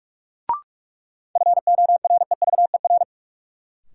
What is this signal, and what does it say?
Morse code, heard as a keyed tone, about 46 wpm, 700 Hz. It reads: FOREVER
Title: Sentences from Top 1600 Words - New Word Drill - 45wpm